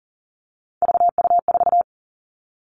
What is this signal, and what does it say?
Morse code, heard as a keyed tone, about 40 wpm, 700 Hz. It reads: VU4
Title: All Cut Numbers - 3-Digit 40wpm